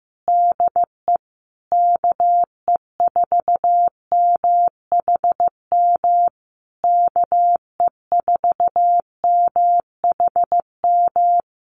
Morse code, heard as a keyed tone, about 15 wpm, 700 Hz. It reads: DE KE4MHM KE4MHM
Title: Common QSO Phrases Encore 15wpm